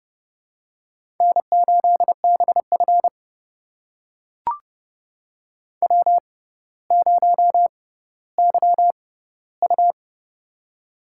Morse code, heard as a keyed tone, about 32 wpm, spaced as for 16 wpm, 700 Hz. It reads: N8BF W0YU
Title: USA Call Signs - 4 Characters - 6x Character Spacing - 30wpm